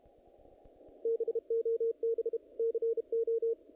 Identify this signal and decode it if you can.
Morse code, heard as a keyed tone, about 32 wpm, 460 Hz, with radio noise. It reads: BOB CO
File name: contests / naqp-cw-k5tr-op-k5pi- -2000